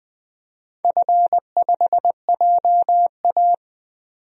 Morse code, heard as a keyed tone, about 20 wpm, 700 Hz. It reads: F5JA
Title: DXCC-100 Call Signs - 4 Characters - 20wpm